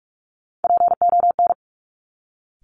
Morse code, heard as a keyed tone, about 46 wpm, 700 Hz. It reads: PON
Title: Top Three Letter Combinations in Words 45wpm